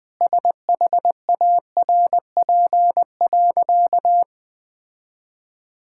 Morse code, heard as a keyed tone, about 20 wpm, 700 Hz. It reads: SHARP.
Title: Sentences from Top 1600 Words 20wpm